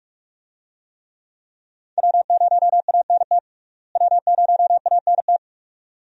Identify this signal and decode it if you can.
Morse code, heard as a keyed tone, about 46 wpm, 700 Hz. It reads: W0ANT W0ANT
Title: USA Call Signs - 5 Characters - Repeated 2x - 45wpm